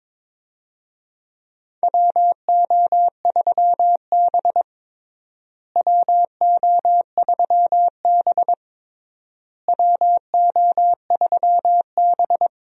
Morse code, heard as a keyed tone, about 22 wpm, 700 Hz. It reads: WO3B WO3B WO3B
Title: USA Call Signs - 4 Characters - Repeated 3x - 22wpm